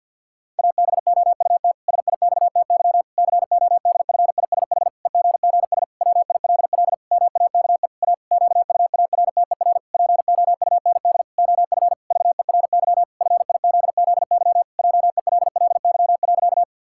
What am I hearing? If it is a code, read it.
Morse code, heard as a keyed tone, about 50 wpm, 700 Hz. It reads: ABOUT SIXTY CODFISH EGGS WILL MAKE A QUARTER POUND OF VERY FIZZY JELLY.
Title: Pangram Challenge v2 - 50wpm